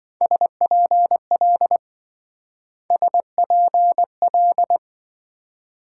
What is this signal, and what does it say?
Morse code, heard as a keyed tone, about 27 wpm, 700 Hz. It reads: SPL SPL